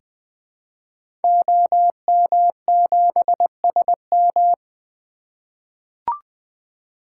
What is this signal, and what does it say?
Morse code, heard as a keyed tone, about 20 wpm, 700 Hz. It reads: OM7SM
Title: 2023 SOTA Activator Call Signs - 20wpm